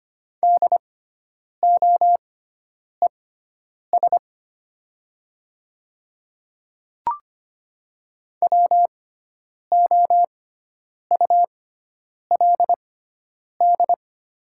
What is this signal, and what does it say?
Morse code, heard as a keyed tone, about 25 wpm, spaced as for 9 wpm, 700 Hz. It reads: DOES WOULD
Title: 6x Character Spacing - Top 300 Words - 25wpm